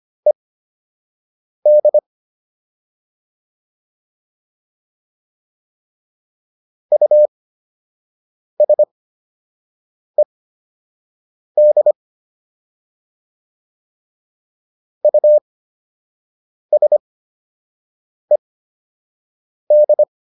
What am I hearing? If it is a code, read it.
Morse code, heard as a keyed tone, about 25 wpm, spaced as for 6 wpm, 600 Hz. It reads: ED USED USED